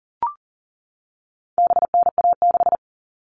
Morse code, heard as a keyed tone, about 40 wpm, 700 Hz. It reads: BNA6
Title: All Cut Numbers - 4-Digit 40wpm